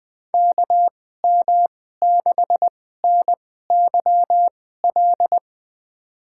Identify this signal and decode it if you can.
Morse code, heard as a keyed tone, about 20 wpm, spaced as for 15 wpm, 700 Hz. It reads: KM6NYL
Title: USA Call Signs - 6 Characters - 2x Character Spacing - 20wpm